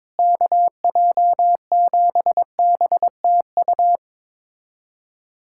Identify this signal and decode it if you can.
Morse code, heard as a keyed tone, about 22 wpm, 700 Hz. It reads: KJ7BTU
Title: USA Callsign DE Callsign - 6 Characters - 22wpm